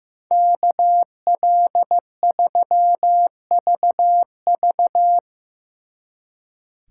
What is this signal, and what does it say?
Morse code, heard as a keyed tone, about 17 wpm, 700 Hz. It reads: KL3VV
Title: Speed Racing - USA 2x2 Call Signs 15wpm